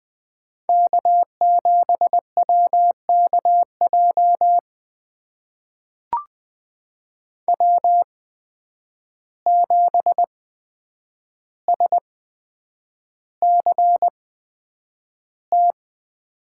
Morse code, heard as a keyed tone, about 20 wpm, spaced as for 9 wpm, 700 Hz. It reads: K7WKJ W7SCT
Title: USA Call Signs - 5 Characters - 8x Character Spacing - 20wpm